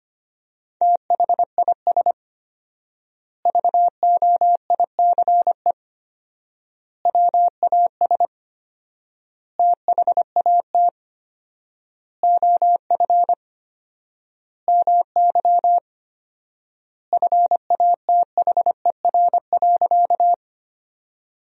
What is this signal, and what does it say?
Morse code, heard as a keyed tone, about 25 wpm, 700 Hz. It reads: THIS VOICE WAS THAT OF MY FATHER.
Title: 4x Word-Spacing - Sentences from Top 500 Words - 25wpm